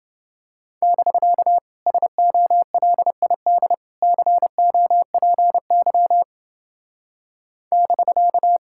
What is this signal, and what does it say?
Morse code, heard as a keyed tone, about 28 wpm, 700 Hz. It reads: <BK> SOLID COPY <BK>